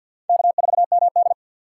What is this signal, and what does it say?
Morse code, heard as a keyed tone, about 52 wpm, 700 Hz. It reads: K4MD
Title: USA Call Signs - 4 Characters - 6x Character Spacing - 50wpm